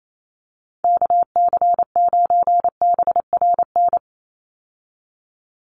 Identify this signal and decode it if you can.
Morse code, heard as a keyed tone, about 28 wpm, 700 Hz. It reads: KC9BRN
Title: USA Call Signs 28wpm